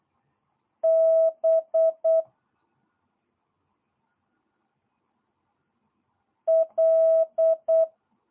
Morse code, heard as a keyed tone, about 8 wpm, 650 Hz. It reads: B L
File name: jh8cbh / 2022cw